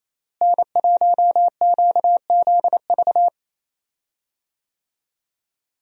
Morse code, heard as a keyed tone, about 28 wpm, 700 Hz. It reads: N1QZV
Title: USA Call Signs - 5 Characters - Repeated 2x - 28wpm